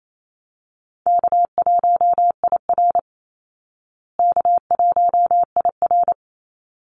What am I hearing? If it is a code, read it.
Morse code, heard as a keyed tone, about 28 wpm, 700 Hz. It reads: K1IR K1IR
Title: Call Signs Repeated 3x - 28wpm